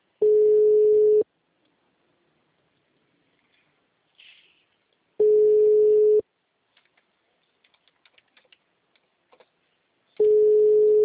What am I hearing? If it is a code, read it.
Ringback tone (Europe, 425 Hz).